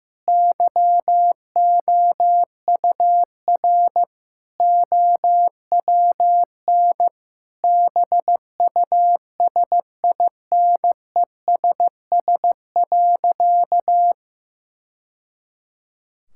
Morse code, heard as a keyed tone, about 15 wpm, 700 Hz. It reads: YOUR OWN BUSINESS.
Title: Sentences from Top 1000 Words - Review Part 5 - 15wpm